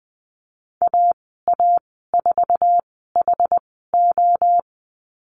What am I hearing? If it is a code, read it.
Morse code, heard as a keyed tone, about 20 wpm, spaced as for 15 wpm, 700 Hz. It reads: AA4HO
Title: 076 Lesson 13g - Letter L - Call Signs 20wpm